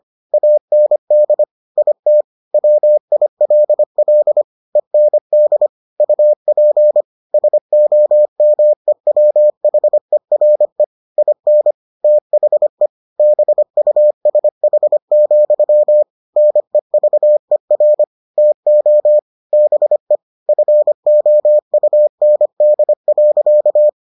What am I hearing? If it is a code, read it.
Morse code, heard as a keyed tone, about 25 wpm, 600 Hz. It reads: AND IT WILL END UP SOMEWHERE IN THE BUSH, NEVER TO BE FOUND.